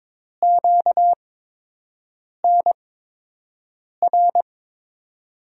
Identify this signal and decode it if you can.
Morse code, heard as a keyed tone, about 22 wpm, spaced as for 6 wpm, 700 Hz. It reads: QNR
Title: USA Call Signs - 8x Character Spacing - 22wpm